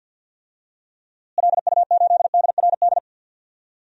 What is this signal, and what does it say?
Morse code, heard as a keyed tone, about 46 wpm, 700 Hz. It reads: LU8DRD
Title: DXCC-100 Call Signs - 6 Characters - Repeated 3x - 50wpm